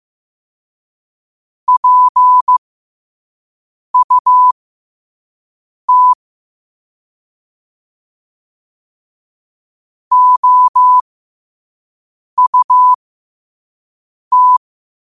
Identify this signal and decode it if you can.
Morse code, heard as a keyed tone, about 15 wpm, spaced as for 5 wpm, 1000 Hz. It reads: PUT OUT
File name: 2026-03-16-5wpm